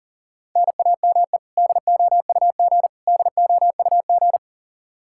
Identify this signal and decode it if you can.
Morse code, heard as a keyed tone, about 40 wpm, 700 Hz. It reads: NAME DOUG DOUG
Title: Practice QSO - Exchange 1 - Beginning - 40wpm